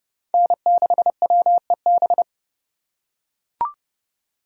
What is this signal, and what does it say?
Morse code, heard as a keyed tone, about 29 wpm, 700 Hz. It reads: N6WEB